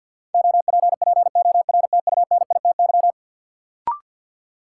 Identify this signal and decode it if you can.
Morse code, heard as a keyed tone, about 52 wpm, 700 Hz. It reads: OPPORTUNITY
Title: Sentences from Top 1000 Words - New Word Drill - 50wpm